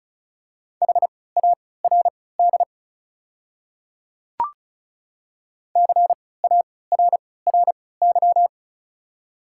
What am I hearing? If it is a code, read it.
Morse code, heard as a keyed tone, about 34 wpm, spaced as for 20 wpm, 700 Hz. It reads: HARD CARRY
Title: 3x Character Spacing - Top 300 Words - 35wpm